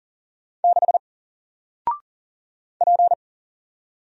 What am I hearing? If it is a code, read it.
Morse code, heard as a keyed tone, about 40 wpm, 700 Hz. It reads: 6 P
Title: Single Letter-Number Alt 40wpm